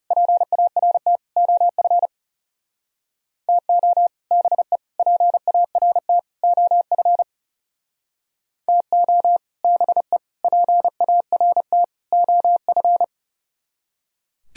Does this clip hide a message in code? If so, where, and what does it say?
Yes: Morse code from 0.1 to 13.0 s, heard as a keyed tone, about 39 wpm, 700 Hz. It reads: PART OF TO BE PART OF TO BE PART OF